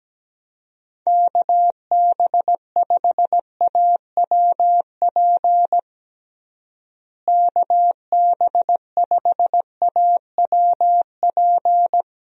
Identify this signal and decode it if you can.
Morse code, heard as a keyed tone, about 17 wpm, 700 Hz. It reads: KB5AWP KB5AWP
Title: USA Call Signs - 6 Characters - Repeated 2x - 17wpm